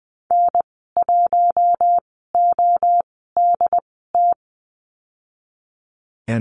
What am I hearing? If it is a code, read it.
Morse code, heard as a keyed tone, about 20 wpm, spaced as for 14 wpm, 700 Hz. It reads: N1ODT
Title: 128 Lesson 21g - Number 6 - Call Signs 20wpm